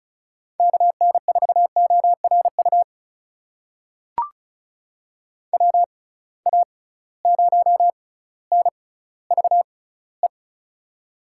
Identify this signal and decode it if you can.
Morse code, heard as a keyed tone, about 34 wpm, spaced as for 18 wpm, 700 Hz. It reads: KN4ORU WA0NVE
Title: USA Call Signs - 6 Characters - 6x Character Spacing - 35wpm